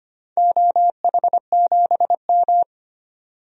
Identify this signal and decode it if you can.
Morse code, heard as a keyed tone, about 25 wpm, 700 Hz. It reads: OH7M
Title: DXCC-100 Call Signs - 4 Characters - 25wpm